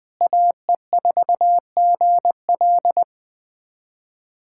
Morse code, heard as a keyed tone, about 20 wpm, 700 Hz. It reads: AE4GL
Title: USA Callsign DE Callsign - 5 Characters - 20wpm